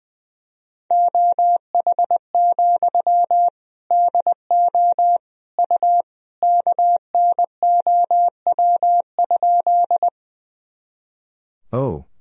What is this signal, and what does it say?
Morse code, heard as a keyed tone, about 20 wpm, 700 Hz. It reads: OH, DO U KNOW?
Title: Sentences from Top 500 Words and Abbreviations 20wpm